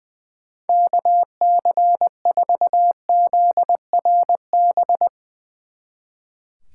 Morse code, heard as a keyed tone, about 22 wpm, 700 Hz. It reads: KC4ZRB
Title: Speed Racing - USA 2x3 Call Signs 20wpm